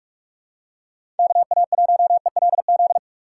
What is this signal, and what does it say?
Morse code, heard as a keyed tone, about 46 wpm, 700 Hz. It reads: KA1ELZ
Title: USA Call Signs - 6x Character Spacing - 45wpm